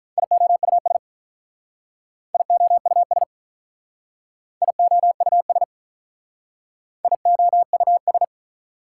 Morse code, heard as a keyed tone, about 45 wpm, 700 Hz. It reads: IOUS IOUS IOUS IOUS